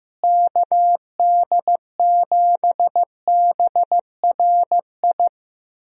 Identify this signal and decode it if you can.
Morse code, heard as a keyed tone, about 15 wpm, 700 Hz. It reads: KD7BRI
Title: USA Call Signs - 6 Characters - Repeated 3x - 15wpm